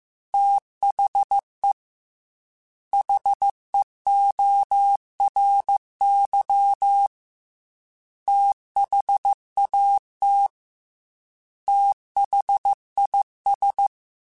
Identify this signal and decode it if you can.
Morse code, heard as a keyed tone, about 15 wpm, 800 Hz. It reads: THE HEORY THAT THIS
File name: TH Runs / TH Runs mp3s / TH Runs @13